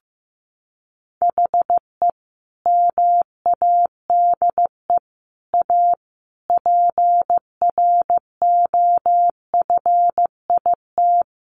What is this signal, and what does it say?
Morse code, heard as a keyed tone, about 15 wpm, 700 Hz. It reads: HE MADE A PROFIT
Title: Sentences from Top 800 Words 15wpm